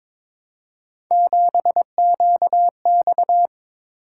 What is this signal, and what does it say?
Morse code, heard as a keyed tone, about 22 wpm, 700 Hz. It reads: 7QX
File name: practice / 22wpm / Three Letter-Number Combinations - 22wpm